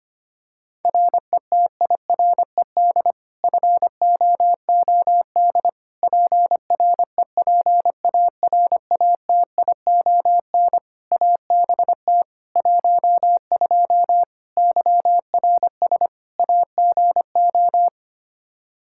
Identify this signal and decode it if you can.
Morse code, heard as a keyed tone, about 25 wpm, 700 Hz. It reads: RETIRED FOOD PREPARATION ABT 12 YRS AGO
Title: K1USN SST QSO Practice 25wpm